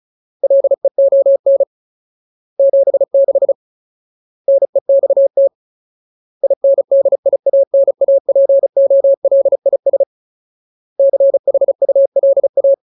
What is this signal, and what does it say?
Morse code, heard as a keyed tone, about 34 wpm, 550 Hz. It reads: LEON 76 NEXT INDIANAPOLIS CHULA
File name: Ultimate Word List / 35WPM Ultimate